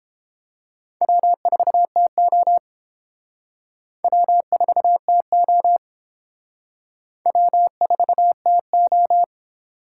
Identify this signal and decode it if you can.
Morse code, heard as a keyed tone, about 30 wpm, 700 Hz. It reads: W4TO W4TO W4TO